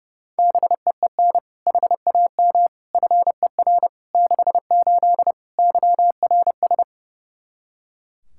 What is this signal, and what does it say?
Morse code, heard as a keyed tone, about 30 wpm, 700 Hz. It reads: BEEN HAM FER 68 YRS
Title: Practice QSO - Exchange 3 - More Info - 30wpm